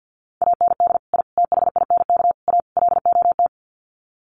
Morse code, heard as a keyed tone, about 50 wpm, 700 Hz. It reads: AND I THINK A LOT
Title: Sets of 5 Words Encore2 50wpm